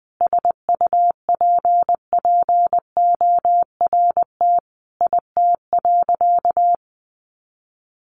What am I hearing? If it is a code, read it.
Morse code, heard as a keyed tone, about 20 wpm, 700 Hz. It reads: SUPPORT IT.
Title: Sentences from Top 1300 Words 20wpm